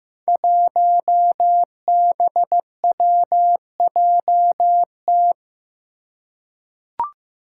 Morse code, heard as a keyed tone, about 15 wpm, 700 Hz. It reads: 1BWJT
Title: Five Challenging Letter-Number Combinations 15wpm